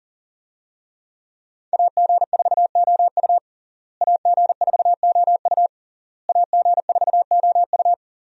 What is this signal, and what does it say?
Morse code, heard as a keyed tone, about 40 wpm, 700 Hz. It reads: AG4OU AG4OU AG4OU
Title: USA Call Signs - 5 Characters - Repeated 3x - 40wpm